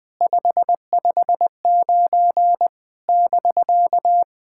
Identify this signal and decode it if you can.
Morse code, heard as a keyed tone, about 21 wpm, 700 Hz. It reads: 559 <BK>
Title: SOTA QSO Practice 20wpm